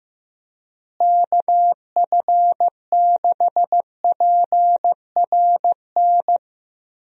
Morse code, heard as a keyed tone, about 15 wpm, 700 Hz. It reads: KF6PRN
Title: USA Call Signs - 6 Characters - Repeated 2x - 15wpm